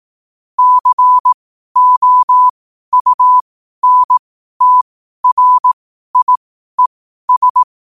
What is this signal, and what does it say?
Morse code, heard as a keyed tone, about 18 wpm, spaced as for 12 wpm, 1000 Hz. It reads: COUNTRIES
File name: qotdfiles / 12wpm